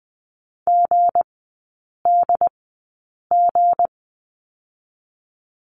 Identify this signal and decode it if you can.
Morse code, heard as a keyed tone, about 20 wpm, spaced as for 8 wpm, 700 Hz. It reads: GDG
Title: 184 Lesson 30e - Letter b - Three Characters 20wpm